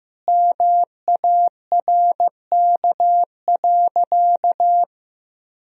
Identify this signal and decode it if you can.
Morse code, heard as a keyed tone, about 15 wpm, 700 Hz. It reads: MARK.
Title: Sentences from Top 1000 Words - Review Part 2 - 15wpm